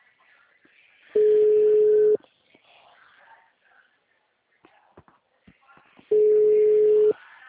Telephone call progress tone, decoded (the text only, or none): ringback tone